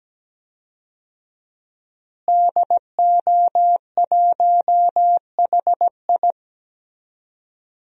Morse code, heard as a keyed tone, about 17 wpm, 700 Hz. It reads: DO1HI
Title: DXCC-100 Call Signs - 5 Characters - Repeated 2x - 17wpm